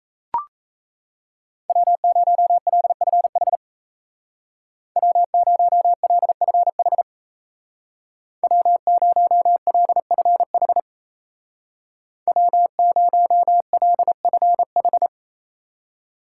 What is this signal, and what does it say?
Morse code, heard as a keyed tone, about 35 wpm, 700 Hz. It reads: W0LFH W0LFH W0LFH W0LFH
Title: Speed Racing - USA Call Signs 28wpm